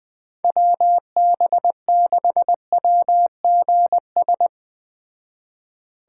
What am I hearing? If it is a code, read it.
Morse code, heard as a keyed tone, about 20 wpm, 700 Hz. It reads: WB6WGS
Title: Speed Racing - USA 2x3 Call Signs 17wpm